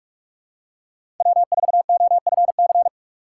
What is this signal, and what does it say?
Morse code, heard as a keyed tone, about 46 wpm, 700 Hz. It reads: W4OFC
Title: USA Call Signs - 5 Characters - 3x Character Spacing - 45wpm